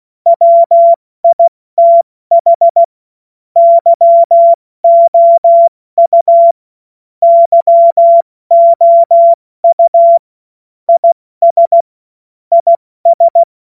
Morse code, heard as a keyed tone, about 16 wpm, 680 Hz. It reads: WITH YOU YOU IS IS
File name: Morsekurs / Witze / Witz 23 Januar 2024 70 F80